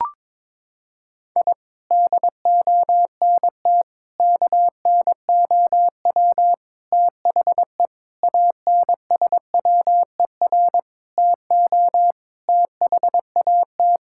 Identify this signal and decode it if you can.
Morse code, heard as a keyed tone, about 22 wpm, 700 Hz. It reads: I DONT KNOW THE ANSWER TO THAT
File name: practice / 22wpm / Sentences with 7-Word Phrases - 22wpm